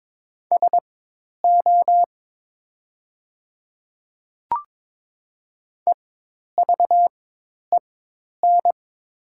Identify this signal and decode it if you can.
Morse code, heard as a keyed tone, about 22 wpm, spaced as for 10 wpm, 700 Hz. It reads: SO EVEN
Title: 4x Character Spacing - Top 200 Words - 22wpm